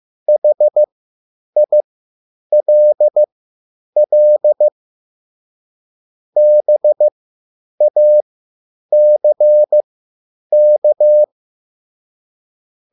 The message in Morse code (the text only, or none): HILL BACK